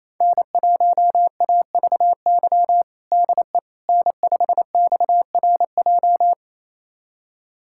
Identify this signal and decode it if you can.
Morse code, heard as a keyed tone, about 28 wpm, 700 Hz. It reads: N1AVY DE N5XRJ
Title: USA Callsign DE Callsign - 5 Characters - 28wpm